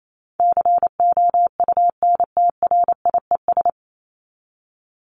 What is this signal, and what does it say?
Morse code, heard as a keyed tone, about 28 wpm, 700 Hz. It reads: COUNTRIES
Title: ES:S Suffix - Top 100 Words with Base Words 28wpm